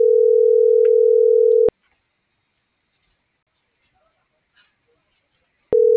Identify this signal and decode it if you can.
Ringback tone (North America), 440 + 480 Hz.